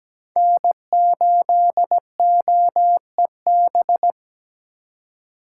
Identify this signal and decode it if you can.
Morse code, heard as a keyed tone, about 17 wpm, 700 Hz. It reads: N8OEB
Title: USA Call Signs - 5 Characters - 4x Character Spacing - 17wpm